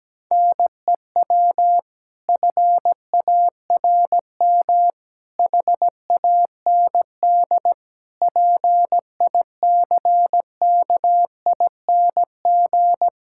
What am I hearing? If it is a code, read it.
Morse code, heard as a keyed tone, about 17 wpm, 700 Hz. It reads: NEW FARM HAND PICKING
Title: Pangram Challenge v2 - 17wpm